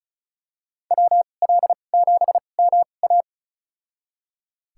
Morse code, heard as a keyed tone, about 36 wpm, spaced as for 25 wpm, 700 Hz. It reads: WL7MA
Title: USA Call Signs - 5 Characters - 2x Character Spacing - 35wpm